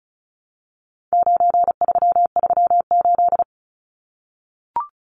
Morse code, heard as a keyed tone, about 34 wpm, 700 Hz. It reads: 9338
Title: Numbers - 4-Digit 35wpm